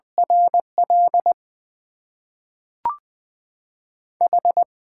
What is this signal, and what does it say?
Morse code, heard as a keyed tone, about 20 wpm, 690 Hz. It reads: RL H
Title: Top 20 QSO Elements - 6x Character Spacing - 20wpm